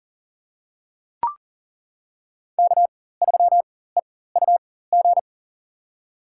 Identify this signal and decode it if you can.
Morse code, heard as a keyed tone, about 37 wpm, spaced as for 18 wpm, 700 Hz. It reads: K3EUG
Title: USA Call Signs - 5 Characters - 4x Character Spacing - 40wpm